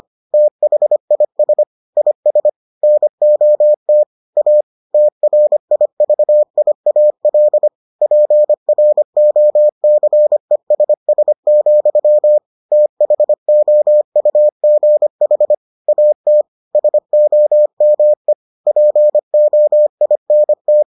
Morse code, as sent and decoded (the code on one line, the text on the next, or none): - .... .. ... / .. ... / -. --- - / .- / - .-. .. ...- .. .- .-.. / .--. .-. --- -.-. ... ... --..-- / - .... --- ..- --. .... / .- - / ... --- -- . / .--. --- .. -. -
THIS IS NOT A TRIVIAL PROCESS, THOUGH AT SOME POINT